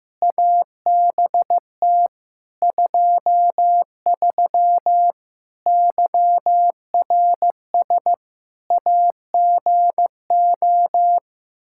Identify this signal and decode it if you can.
Morse code, heard as a keyed tone, about 15 wpm, 700 Hz. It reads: ABT 23 YRS AGO
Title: K1USN SST QSO Practice 15wpm